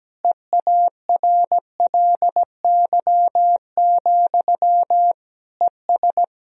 Morse code, heard as a keyed tone, about 17 wpm, 700 Hz. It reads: EARLY, ES
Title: Sentences from Top 400 Words and Abbreviations 17wpm